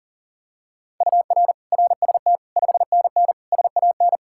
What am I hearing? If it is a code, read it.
Morse code, heard as a keyed tone, about 40 wpm, 700 Hz. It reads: UR RST 5NN SAN
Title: Practice QSO - Exchange 1 - Beginning - 40wpm